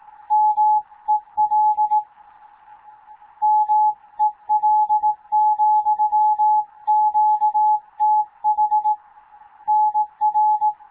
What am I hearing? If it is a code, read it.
Morse code, heard as a keyed tone, about 18 wpm, 850 Hz. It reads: M EL MEL,QTH NR
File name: Morze-HA2MN / hangp / 2 11 GYAKORLAT